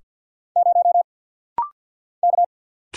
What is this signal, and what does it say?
Morse code, heard as a keyed tone, about 50 wpm, 700 Hz. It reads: 0 K
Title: Single Letter-Number - Rapid-Fire 50wpm